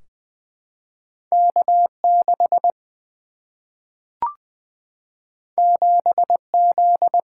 Morse code, heard as a keyed tone, about 20 wpm, 700 Hz. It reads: K 6 7 Z
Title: Single-Letter Single-Number 20wpm